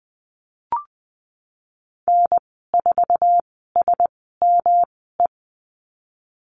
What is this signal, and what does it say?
Morse code, heard as a keyed tone, about 20 wpm, spaced as for 15 wpm, 700 Hz. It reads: N4SME
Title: 154 Lesson 25g - Letter p - Call Signs 20wpm